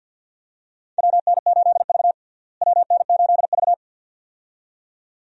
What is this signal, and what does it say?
Morse code, heard as a keyed tone, about 48 wpm, 700 Hz. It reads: WN8V WN8V
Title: USA Call Signs - 4 Characters - Repeated 2x - 50wpm